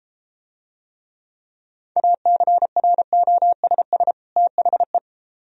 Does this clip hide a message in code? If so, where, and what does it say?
Yes: Morse code from 2.0 to 5.0 s, heard as a keyed tone, about 33 wpm, 700 Hz. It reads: ACROSS THE